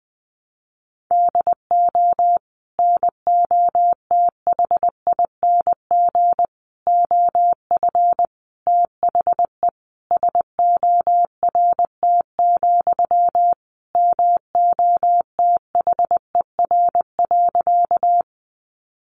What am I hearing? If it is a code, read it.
Morse code, heard as a keyed tone, about 20 wpm, 700 Hz. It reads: DO NOTHING OF THE SORT, MOTHER.
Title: Sentences from Top 1300 Words 20wpm